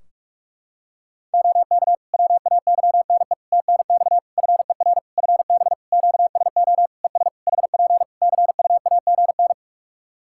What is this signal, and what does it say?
Morse code, heard as a keyed tone, about 46 wpm, 700 Hz. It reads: OK WAYNE TNX FER FB QSO ES HP CUAGN
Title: Practice QSO - Exchange 4 - Ending - 45wpm